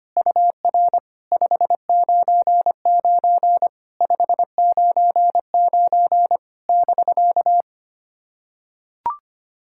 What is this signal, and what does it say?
Morse code, heard as a keyed tone, about 25 wpm, 690 Hz. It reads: UR 599 599 <BK>
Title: SOTA QSO Practice 25wpm